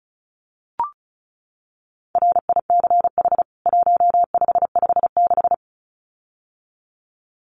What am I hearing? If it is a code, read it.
Morse code, heard as a keyed tone, about 34 wpm, 700 Hz. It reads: RICH 1556
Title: CWT Exchange 35wpm